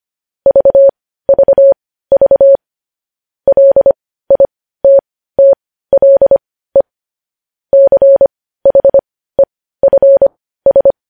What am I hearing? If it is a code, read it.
Morse code, heard as a keyed tone, about 25 wpm, spaced as for 15 wpm, 550 Hz. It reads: VVV LITTLE CHEFS
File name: podcast / News headlines-1x-25x15